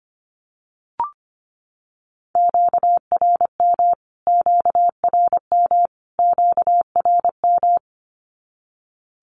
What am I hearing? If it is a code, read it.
Morse code, heard as a keyed tone, about 25 wpm, 700 Hz. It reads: QRM QRM QRM
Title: Q-Codes Repeated 3x 25wpm